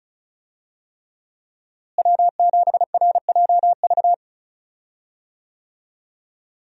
Morse code, heard as a keyed tone, about 36 wpm, 700 Hz. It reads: W7RJV
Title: USA Call Signs - 5 Characters - Repeated 3x - 35wpm